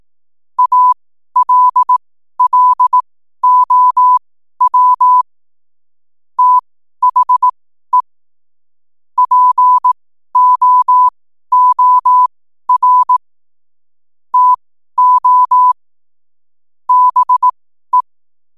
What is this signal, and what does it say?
Morse code, heard as a keyed tone, about 18 wpm, spaced as for 12 wpm, 1000 Hz. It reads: ALLOW THE POOR TO BE
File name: n7net / 12wpm